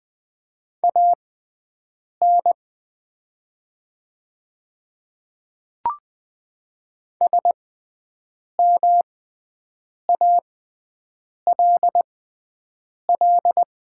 Morse code, heard as a keyed tone, about 20 wpm, spaced as for 7 wpm, 700 Hz. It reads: AN SMALL
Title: 6x Character Spacing - Top 200 Words - 20wpm